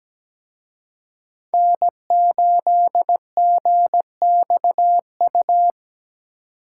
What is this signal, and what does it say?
Morse code, heard as a keyed tone, about 20 wpm, 700 Hz. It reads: N8GXU
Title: Speed Racing - USA 1x3 Call Signs 17wpm